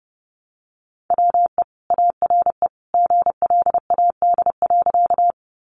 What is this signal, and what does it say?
Morse code, heard as a keyed tone, about 30 wpm, 700 Hz. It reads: WE ARE GLAD.
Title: Sentences from Top 800 Words 30wpm